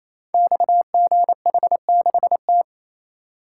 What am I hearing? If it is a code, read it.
Morse code, heard as a keyed tone, about 27 wpm, 700 Hz. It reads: XGH6T
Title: Code Groups - 5 Letter:Numbers - 28wpm